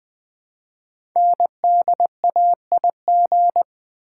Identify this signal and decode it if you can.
Morse code, heard as a keyed tone, about 20 wpm, 700 Hz. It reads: NDAIG
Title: Code Groups - 5 Letters - 20wpm